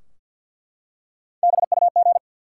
Morse code, heard as a keyed tone, about 48 wpm, 700 Hz. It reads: BUG
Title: Three Letter Words - 50wpm